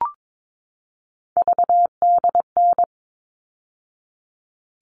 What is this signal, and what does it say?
Morse code, heard as a keyed tone, about 22 wpm, 700 Hz. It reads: VDN
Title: All Cut Numbers - 3-Digit 22wpm